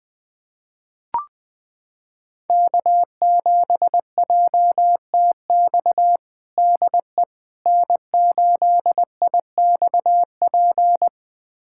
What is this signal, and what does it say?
Morse code, heard as a keyed tone, about 20 wpm, 700 Hz. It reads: K7JTX DE N8IXP
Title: USA Callsign DE Callsign - 5 Characters - 20wpm